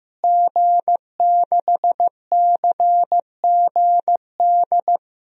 Morse code, heard as a keyed tone, about 15 wpm, 700 Hz. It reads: G6CGD
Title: DXCC-100 Call Signs - 5 Characters - Repeated 2x - 15wpm